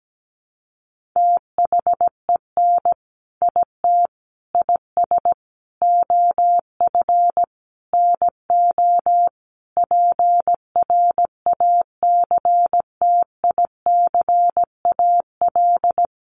Morse code, heard as a keyed tone, about 17 wpm, 700 Hz. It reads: THEN IT IS OF NO PRACTICAL
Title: Sentences from Top 1400 Words 17wpm